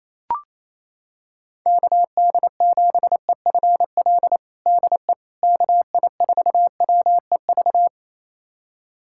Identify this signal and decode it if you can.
Morse code, heard as a keyed tone, about 28 wpm, 700 Hz. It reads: KD7EFL DE KI4WEV
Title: USA Callsign DE Callsign - 6 Characters - 28wpm